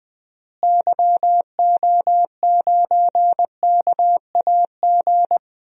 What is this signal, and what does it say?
Morse code, heard as a keyed tone, about 21 wpm, 700 Hz. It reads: YO9KAG
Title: DXCC-100 Call Signs - 6 Characters - Repeated 3x - 20wpm